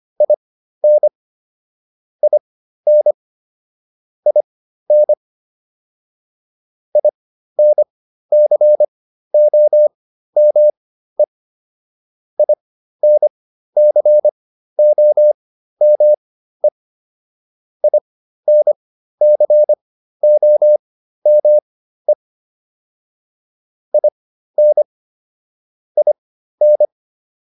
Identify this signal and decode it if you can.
Morse code, heard as a keyed tone, about 25 wpm, spaced as for 13 wpm, 600 Hz. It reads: IN IN IN INCOME INCOME INCOME IN IN